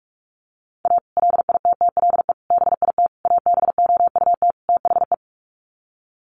Morse code, heard as a keyed tone, about 44 wpm, 700 Hz. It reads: A LITTLE BIT ABOUT THE
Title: Sets of 5 Words Encore2 45wpm